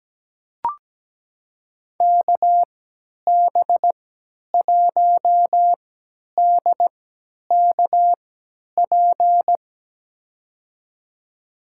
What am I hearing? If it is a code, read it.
Morse code, heard as a keyed tone, about 17 wpm, spaced as for 10 wpm, 700 Hz. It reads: KB1DKP
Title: USA Call Signs - 6 Characters - 3x Character Spacing - 17wpm